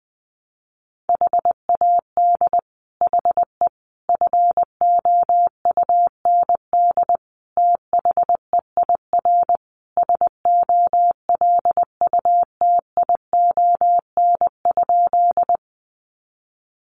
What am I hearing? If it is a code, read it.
Morse code, heard as a keyed tone, about 19 wpm, 700 Hz. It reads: HAD HE FOUND THEIR SOLUTION?
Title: Sentences from Top 1300 Words 20wpm